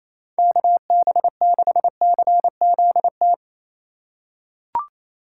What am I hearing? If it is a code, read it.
Morse code, heard as a keyed tone, about 28 wpm, 700 Hz. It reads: KB6CZT
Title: USA Call Signs - 6 Characters - 2x Character Spacing - 28wpm